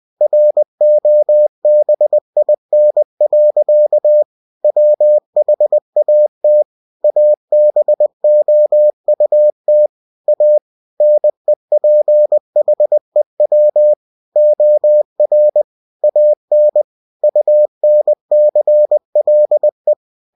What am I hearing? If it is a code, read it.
Morse code, heard as a keyed tone, about 20 wpm, 600 Hz. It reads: ROBIN. WHAT ABOUT A NEPHEW OR AN UNCLE
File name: cw-books / winnie-the-pooh / winnie-20wpm-page-0074